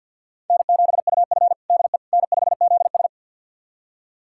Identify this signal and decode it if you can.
Morse code, heard as a keyed tone, about 49 wpm, 700 Hz. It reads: N7UF DE N5ZS
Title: USA Callsign DE Callsign - 4 Characters - 50wpm